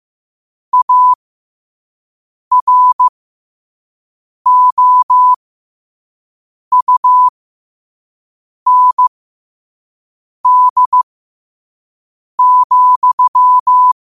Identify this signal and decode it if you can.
Morse code, heard as a keyed tone, about 15 wpm, spaced as for 5 wpm, 1000 Hz. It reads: AROUND,
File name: qotdfiles / 5wpm